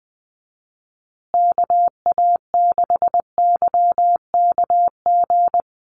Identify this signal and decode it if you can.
Morse code, heard as a keyed tone, about 20 wpm, 700 Hz. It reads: KA6YKG
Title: USA Call Signs Repeated 2x 20wpm